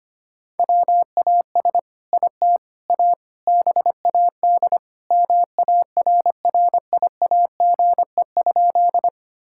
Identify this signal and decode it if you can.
Morse code, heard as a keyed tone, about 25 wpm, 700 Hz. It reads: WAS IT A BAD MARRIAGE?
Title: Sentences from Top 1000 Words - Review Part 5 - 25wpm